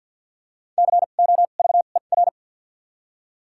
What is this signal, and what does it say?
Morse code, heard as a keyed tone, about 52 wpm, spaced as for 37 wpm, 700 Hz. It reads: COVER
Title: 2x Character Spacing - Top 300 Words - 50wpm